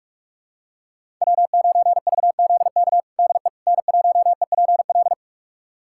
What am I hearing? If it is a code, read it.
Morse code, heard as a keyed tone, about 46 wpm, 700 Hz. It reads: W9VZK DE N1EPL
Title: USA Callsign DE Callsign - 5 Characters - 45wpm